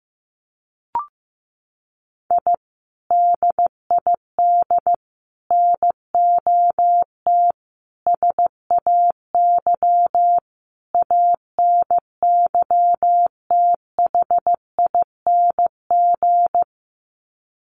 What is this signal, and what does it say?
Morse code, heard as a keyed tone, about 15 wpm, 700 Hz. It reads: I DID NOT SAY ANYTHING
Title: Sets of 5 Words Encore2 15wpm